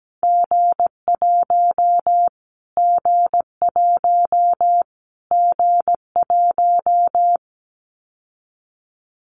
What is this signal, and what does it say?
Morse code, heard as a keyed tone, about 17 wpm, 700 Hz. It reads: G1 G1 G1
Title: Callsign Prefixes for Top 25 DXCC - Repeated 3x - 17wpm